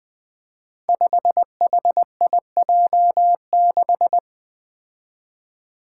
Morse code, heard as a keyed tone, about 20 wpm, 700 Hz. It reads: 5HIJ6
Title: Five Challenging Letter-Number Combinations 20wpm